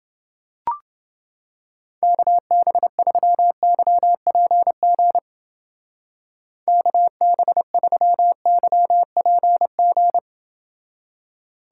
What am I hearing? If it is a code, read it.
Morse code, heard as a keyed tone, about 29 wpm, 700 Hz. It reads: KB3YPG KB3YPG